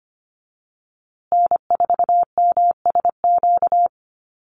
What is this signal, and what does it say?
Morse code, heard as a keyed tone, about 25 wpm, 700 Hz. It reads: N4MSQ
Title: USA Call Signs Repeated 2x 25wpm